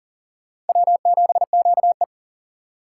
Morse code, heard as a keyed tone, about 51 wpm, 700 Hz. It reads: W7QE